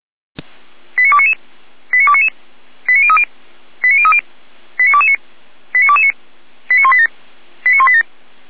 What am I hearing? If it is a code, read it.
Five-tone selective call (ZVEI), heard as quick runs of five tones, tones of about 70 ms. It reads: 89299 89299 89030 89030 89229 89229 89187 89187